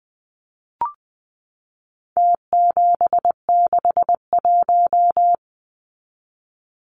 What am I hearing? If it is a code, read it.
Morse code, heard as a keyed tone, about 20 wpm, 700 Hz. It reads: T761
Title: Numbers - Contest Style - 4-Digit 20wpm